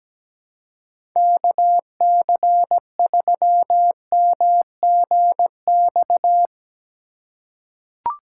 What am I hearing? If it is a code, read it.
Morse code, heard as a keyed tone, about 17 wpm, 700 Hz. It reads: KC3MGX
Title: USA Call Signs - 6 Characters - 2x Character Spacing - 17wpm